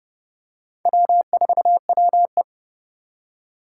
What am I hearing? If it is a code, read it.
Morse code, heard as a keyed tone, about 32 wpm, 700 Hz. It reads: W4WE
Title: Four Challenging Letter-Number Combinations 30wpm